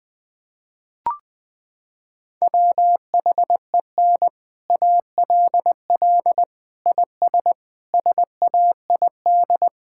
Morse code, heard as a keyed tone, about 20 wpm, 700 Hz. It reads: WHEN ALL IS SAID
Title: Sets of 6 Words - 20wpm